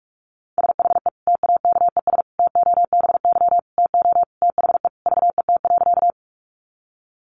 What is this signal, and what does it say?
Morse code, heard as a keyed tone, about 44 wpm, 700 Hz. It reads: SHE TAKES TOBY TO THE VET.
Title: Easy Phrases 45wpm